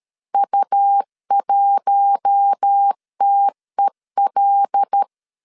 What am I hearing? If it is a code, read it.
Morse code, heard as a keyed tone, about 13 wpm, 800 Hz. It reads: U1TEL
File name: CW / DET / 1 6 1500C S A 12wpm 3440